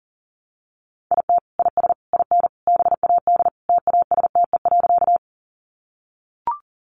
Morse code, heard as a keyed tone, about 40 wpm, 700 Hz. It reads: IT IS IN BAD TASTE.
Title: Sentences from Top 1100 Words 40wpm